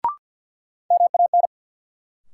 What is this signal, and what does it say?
Morse code, heard as a keyed tone, about 51 wpm, 700 Hz. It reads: MAN
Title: Sentences from Top 200 Words - New Word Drill - 50wpm